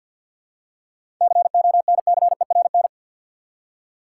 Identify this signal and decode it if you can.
Morse code, heard as a keyed tone, about 51 wpm, 700 Hz. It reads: CONCERN
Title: Seven Letter Words - 50wpm